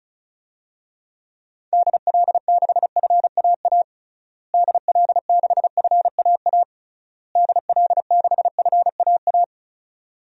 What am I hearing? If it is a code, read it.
Morse code, heard as a keyed tone, about 34 wpm, 700 Hz. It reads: DL6FAA DL6FAA DL6FAA
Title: DXCC-100 Call Signs - 6 Characters - Repeated 3x - 35wpm